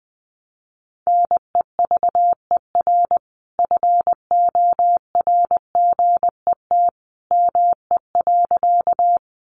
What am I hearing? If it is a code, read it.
Morse code, heard as a keyed tone, about 20 wpm, 700 Hz. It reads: NEVER FORGET ME.
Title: Sentences from Top 1200 Words 20wpm